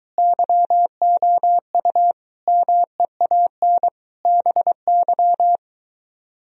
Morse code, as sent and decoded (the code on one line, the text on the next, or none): -.-- --- ..- / -- . .- -. / -... -.--
YOU MEAN BY